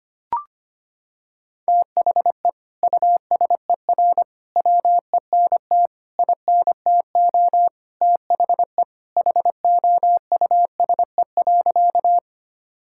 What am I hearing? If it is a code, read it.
Morse code, heard as a keyed tone, about 25 wpm, 700 Hz. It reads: THE USER WENT INTO THE HOUSE.
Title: Sentences from Top 1000 Words - Review Part 5 - 25wpm